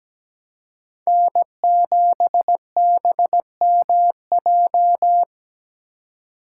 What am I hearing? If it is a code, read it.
Morse code, heard as a keyed tone, about 17 wpm, 700 Hz. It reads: N7BMJ
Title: USA Call Signs - 5 Characters - 3x Character Spacing - 17wpm